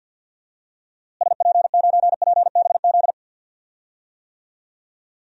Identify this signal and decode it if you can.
Morse code, heard as a keyed tone, about 47 wpm, 700 Hz. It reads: SP9PBZ
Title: DXCC-100 Call Signs - 6 Characters - Repeated 2x - 50wpm